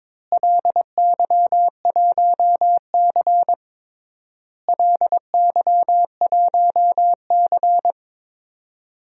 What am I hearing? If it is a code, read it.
Morse code, heard as a keyed tone, about 22 wpm, 700 Hz. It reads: LY1C LY1C
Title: DXCC-100 Call Signs - 4 Characters - Repeated 3x - 22wpm